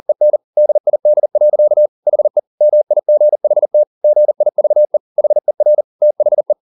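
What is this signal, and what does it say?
Morse code, heard as a keyed tone, about 40 wpm, 600 Hz. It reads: EN DID. HE MIGHT GIVE HER THE
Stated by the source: Morse code at 40 wpm, practice text, CW med Gnister.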